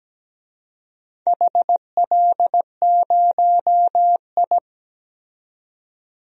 Morse code, heard as a keyed tone, about 17 wpm, 700 Hz. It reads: HL0I
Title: DXCC-100 Call Signs - 4 Characters - 17wpm